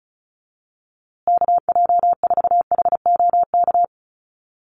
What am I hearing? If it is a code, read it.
Morse code, heard as a keyed tone, about 34 wpm, 700 Hz. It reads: KJ4HOK
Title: USA Call Signs Repeated 2x 35wpm